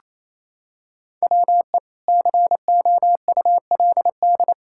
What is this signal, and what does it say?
Morse code, heard as a keyed tone, about 28 wpm, 700 Hz. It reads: WE COULD
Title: Pangram Challenge v2 - 28wpm